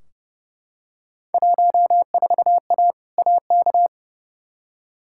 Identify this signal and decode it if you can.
Morse code, heard as a keyed tone, about 30 wpm, 700 Hz. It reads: 14A AK
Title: Field Day Exchanges Encore - 30wpm